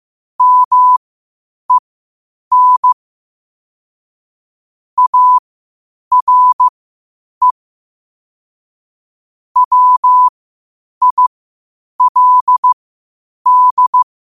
Morse code, heard as a keyed tone, about 15 wpm, spaced as for 8 wpm, 1000 Hz. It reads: MEN ARE WILD